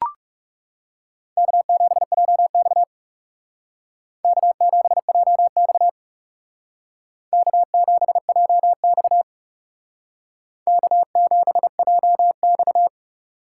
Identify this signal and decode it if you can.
Morse code, heard as a keyed tone, about 36 wpm, 700 Hz. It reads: K7JX K7JX K7JX K7JX
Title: Speed Racing - USA 1x2 Call Signs 30wpm